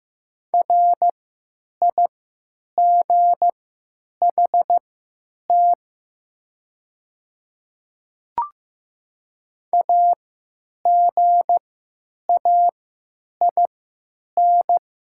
Morse code, heard as a keyed tone, about 15 wpm, spaced as for 9 wpm, 700 Hz. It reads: RIGHT AGAIN
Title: 3x Character Spacing - Top 200 Words - 15wpm